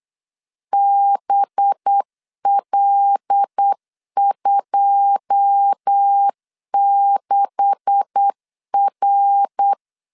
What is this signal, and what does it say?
Morse code, heard as a keyed tone, about 8 wpm, 800 Hz. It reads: BL26R
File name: CW / DET / 1 9 1500C S C 8wpm 5653